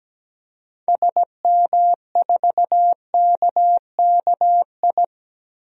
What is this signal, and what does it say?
Morse code, heard as a keyed tone, about 17 wpm, 700 Hz. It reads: SM4KKI
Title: DXCC-100 Call Signs - 6 Characters - 17wpm